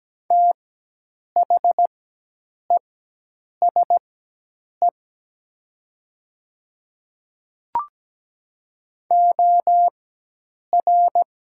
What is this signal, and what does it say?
Morse code, heard as a keyed tone, about 17 wpm, spaced as for 8 wpm, 700 Hz. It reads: THESE OR
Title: 4x Character Spacing - Top 200 Words - 17wpm